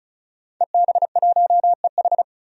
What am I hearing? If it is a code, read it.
Morse code, heard as a keyed tone, about 34 wpm, 700 Hz. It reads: EB1EH
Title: DXCC-100 Call Signs - 5 Characters - 35wpm